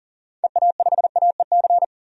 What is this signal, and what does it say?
Morse code, heard as a keyed tone, about 40 wpm, 700 Hz. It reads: EA5AEC
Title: DXCC-100 Call Signs - 6 Characters - 40wpm